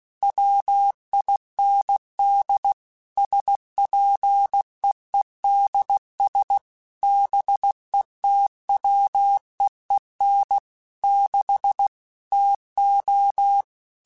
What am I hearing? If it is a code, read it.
Morse code, heard as a keyed tone, about 16 wpm, 800 Hz. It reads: WIND SPEEDS BETWEEN 6 TO
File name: Weather Sets / Weather Sets-mp3s / Weather 4 24 @15